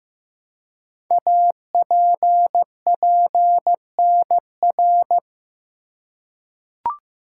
Morse code, heard as a keyed tone, about 15 wpm, 700 Hz. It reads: APPNR
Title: Code Groups - 5 Letter:Numbers - 15wpm